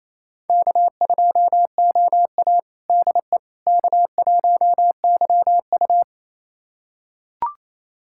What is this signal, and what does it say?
Morse code, heard as a keyed tone, about 28 wpm, 700 Hz. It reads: K2OA DE K1YU
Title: USA Callsign DE Callsign - 4 Characters - 28wpm